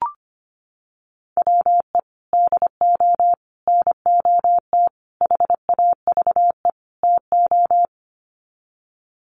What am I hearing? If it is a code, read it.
Morse code, heard as a keyed tone, about 25 wpm, 700 Hz. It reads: WE DO NOT HAVE TO
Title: Sets of 5 Words 25wpm